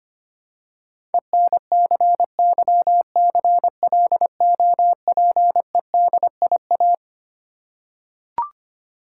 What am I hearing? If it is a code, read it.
Morse code, heard as a keyed tone, about 25 wpm, 700 Hz. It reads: ENCYCLOPEDIA
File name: practice / 25wpm / Difficult Words with Hints - 25wpm